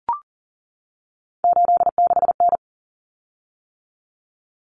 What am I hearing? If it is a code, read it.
Morse code, heard as a keyed tone, about 40 wpm, 700 Hz. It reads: 86N
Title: Common Cut Numbers - 3-Digit 40wpm